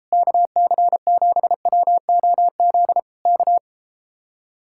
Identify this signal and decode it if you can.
Morse code, heard as a keyed tone, about 33 wpm, 700 Hz. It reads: KC7WOZ K